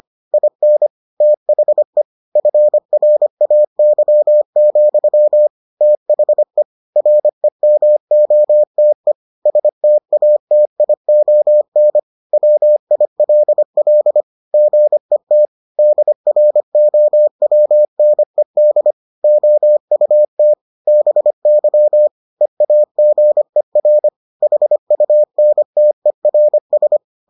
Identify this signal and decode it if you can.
Morse code, heard as a keyed tone, about 25 wpm, 600 Hz. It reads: IN THE FRAY, THE REMOTE STATION WILL GET DROWNED OUT BY EAGER HUNTERS